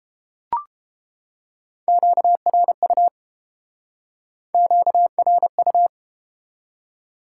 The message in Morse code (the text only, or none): QRU QRU